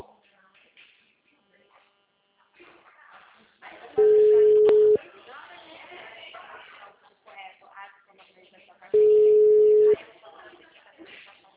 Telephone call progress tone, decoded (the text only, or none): ringback tone